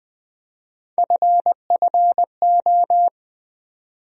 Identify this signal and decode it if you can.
Morse code, heard as a keyed tone, about 20 wpm, 700 Hz. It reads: FFO
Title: USA Call Sign Postfixes - 3 Letters 20wpm